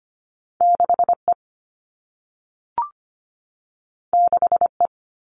Morse code, heard as a keyed tone, about 25 wpm, 700 Hz. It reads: 6E 6E
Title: All Cut Numbers - 2-Digit 25wpm